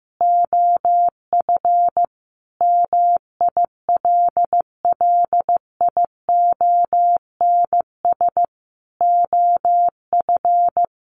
Morse code, heard as a keyed tone, about 14 wpm, 700 Hz. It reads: OF MILLIONS OF
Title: Sets of 5 Words 15wpm